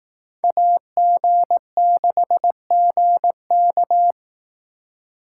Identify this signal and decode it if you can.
Morse code, heard as a keyed tone, about 18 wpm, 700 Hz. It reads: AG6GK
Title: Speed Racing - USA 2x2 Call Signs 15wpm